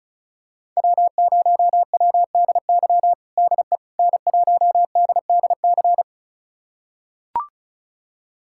Morse code, heard as a keyed tone, about 36 wpm, 690 Hz. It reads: W0WDY DE N1DDC
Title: USA Callsign DE Callsign - 5 Characters - 35wpm